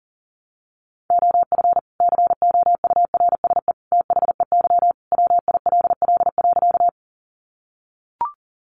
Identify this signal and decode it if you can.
Morse code, heard as a keyed tone, about 40 wpm, 700 Hz. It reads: OF COURSE THEY WILL.
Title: Sentences from Top 500 Words 40wpm